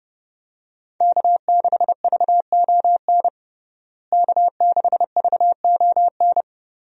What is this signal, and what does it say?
Morse code, heard as a keyed tone, about 32 wpm, 700 Hz. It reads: K6VON K6VON
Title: USA Call Signs - 5 Characters - Repeated 3x - 30wpm